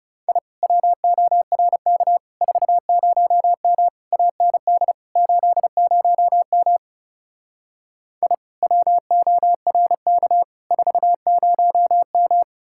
Morse code, heard as a keyed tone, about 38 wpm, 700 Hz. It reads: I WORK 40M AND 80M I WORK 40M